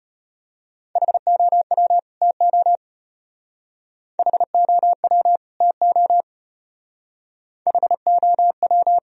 Morse code, heard as a keyed tone, about 31 wpm, 700 Hz. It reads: HOW TO HOW TO HOW
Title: Speed-Racing - Sets of 2 Words 25wpm